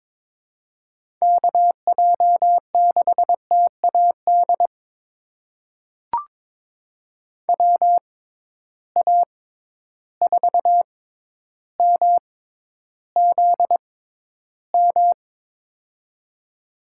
Morse code, heard as a keyed tone, about 22 wpm, spaced as for 11 wpm, 700 Hz. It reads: KJ6TAD WA4MZM
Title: USA Call Signs - 6 Characters - 6x Character Spacing - 22wpm